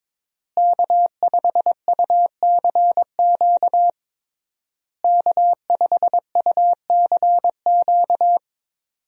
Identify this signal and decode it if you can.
Morse code, heard as a keyed tone, about 22 wpm, 700 Hz. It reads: K5UCQ K5UCQ
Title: USA Call Signs - 5 Characters - Repeated 3x - 22wpm